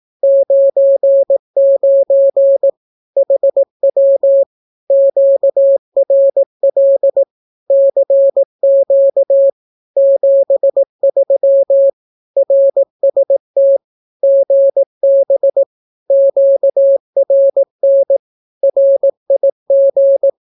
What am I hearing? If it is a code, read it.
Morse code, heard as a keyed tone, about 18 wpm, 550 Hz. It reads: 99 HW QRL CQ 73 RST GB QRN RIG
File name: naukapodstaw / skroty / skroty